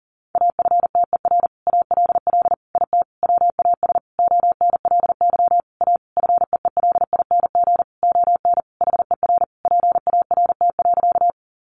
Morse code, heard as a keyed tone, about 40 wpm, 700 Hz. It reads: AFTER ALL IT WAS ONLY A FEELING ON HER PART.
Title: Sentences from Top 1200 Words 40wpm